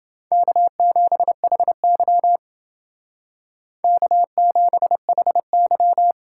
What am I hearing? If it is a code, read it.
Morse code, heard as a keyed tone, about 28 wpm, 700 Hz. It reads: K7HY K7HY